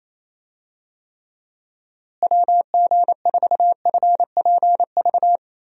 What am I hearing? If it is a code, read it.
Morse code, heard as a keyed tone, about 28 wpm, 700 Hz. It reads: WG4FPV
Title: USA Call Signs - 6 Characters - Repeated 2x - 28wpm